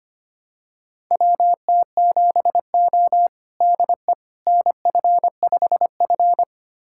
Morse code, heard as a keyed tone, about 25 wpm, 700 Hz. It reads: WT7O DE NF5F
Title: USA Callsign DE Callsign - 4 Characters - 25wpm